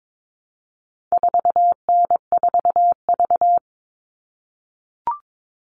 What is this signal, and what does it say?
Morse code, heard as a keyed tone, about 22 wpm, 700 Hz. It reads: 4N4V
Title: All Cut Numbers - 4-Digit 22wpm